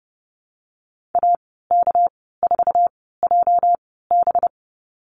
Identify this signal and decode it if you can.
Morse code, heard as a keyed tone, about 30 wpm, spaced as for 17 wpm, 700 Hz. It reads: AK4JB
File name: Lesson 37g - Letter x - Call Signs 30wpm